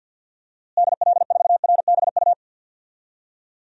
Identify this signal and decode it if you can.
Morse code, heard as a keyed tone, about 45 wpm, 700 Hz. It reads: DL4RBU
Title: DXCC-100 Call Signs - 6 Characters - Repeated 2x - 50wpm